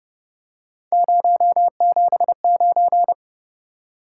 Morse code, heard as a keyed tone, about 33 wpm, 700 Hz. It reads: 079